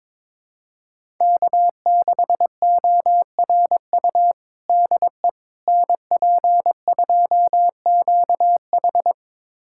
Morse code, heard as a keyed tone, about 22 wpm, 700 Hz. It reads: K6ORU DE NP2QH
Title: USA Callsign DE Callsign - 5 Characters - 22wpm